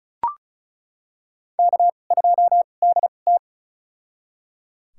Morse code, heard as a keyed tone, about 34 wpm, spaced as for 25 wpm, 700 Hz. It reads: K2DT
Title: USA Call Signs - 2x Character Spacing - 35wpm